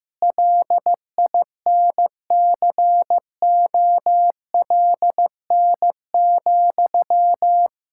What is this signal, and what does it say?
Morse code, heard as a keyed tone, about 16 wpm, 700 Hz. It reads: LINCOLN,
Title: Practice QSO - Exchange 1 - Beginning - 15wpm